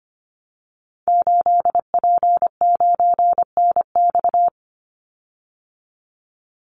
Morse code, heard as a keyed tone, about 25 wpm, 700 Hz. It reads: 8P9NX
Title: Call Signs Repeated 2x - 25wpm